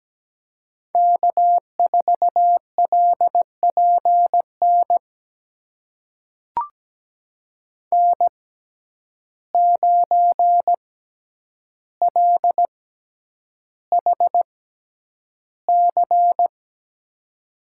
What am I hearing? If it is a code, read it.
Morse code, heard as a keyed tone, about 17 wpm, spaced as for 9 wpm, 700 Hz. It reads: K4LPN N9LHC
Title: USA Call Signs - 5 Characters - 6x Character Spacing - 17wpm